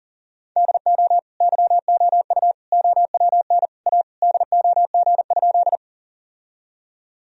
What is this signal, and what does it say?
Morse code, heard as a keyed tone, about 40 wpm, 700 Hz. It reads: DO YOU OWN A DOG?
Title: Sentences from Top 1000 Words - Review Part 2 - 40wpm